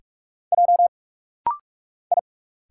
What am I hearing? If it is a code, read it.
Morse code, heard as a keyed tone, about 45 wpm, 690 Hz. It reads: J I
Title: Single Letters - Rapid-Fire 45wpm